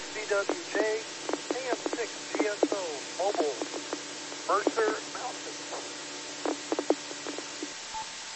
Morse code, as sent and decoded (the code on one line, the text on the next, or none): none